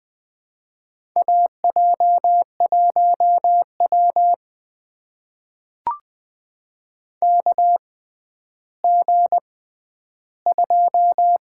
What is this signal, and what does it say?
Morse code, heard as a keyed tone, about 20 wpm, spaced as for 11 wpm, 700 Hz. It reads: AJ1W KG2
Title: USA Call Signs - 4 Characters - 6x Character Spacing - 20wpm